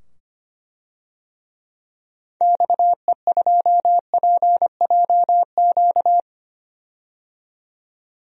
Morse code, heard as a keyed tone, about 25 wpm, 700 Hz. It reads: XE2PJQ
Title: DXCC-100 Call Signs - 6 Characters - 25wpm